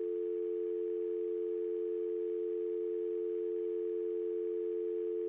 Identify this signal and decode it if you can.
Dial tone (North America), 350 + 440 Hz.
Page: dial-tone